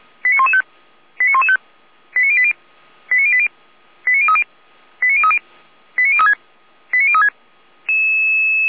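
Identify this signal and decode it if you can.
Five-tone selective call (ZVEI), heard as quick runs of five tones, tones of about 70 ms. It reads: 89115 89115 89080 89080 89030 89030 89036 89036
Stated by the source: ZVEI five-tone call received by SDR.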